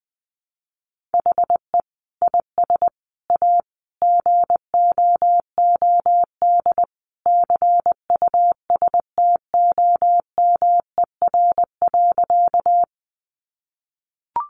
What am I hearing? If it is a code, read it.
Morse code, heard as a keyed tone, about 20 wpm, 700 Hz. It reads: HE IS A GOOD CUSTOMER.
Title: Sentences from Top 1100 Words 20wpm